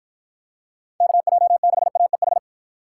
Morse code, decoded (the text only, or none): KJ6AEH